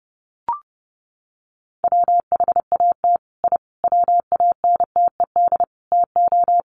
Morse code, heard as a keyed tone, about 30 wpm, 700 Hz. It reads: WHAT I WANTED TO
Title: Sets of 5 Words Encore2 30wpm